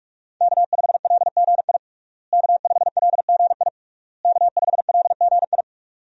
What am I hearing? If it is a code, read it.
Morse code, heard as a keyed tone, about 46 wpm, 700 Hz. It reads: K5LGI K5LGI K5LGI
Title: USA Call Signs - 5 Characters - Repeated 3x - 45wpm